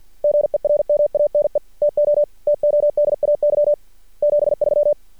Morse code, heard as a keyed tone, about 46 wpm, 600 Hz. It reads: GERMANE TO TODAY 73
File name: archived / CRSnet